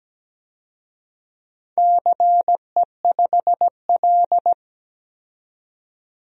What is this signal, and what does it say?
Morse code, heard as a keyed tone, about 17 wpm, 700 Hz. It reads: CE5L
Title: DXCC-100 Call Signs - 4 Characters - 17wpm